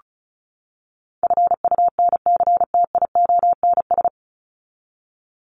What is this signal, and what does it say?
Morse code, heard as a keyed tone, about 34 wpm, 700 Hz. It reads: FUNCTIONS
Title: ES:S Suffix - Top 100 Words with Base Words 35wpm